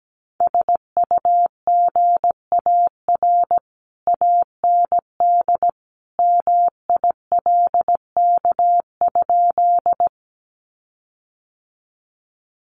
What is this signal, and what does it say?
Morse code, heard as a keyed tone, about 17 wpm, 700 Hz. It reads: SUGAR AND MILK?
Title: Sentences from Top 1500 Words 17wpm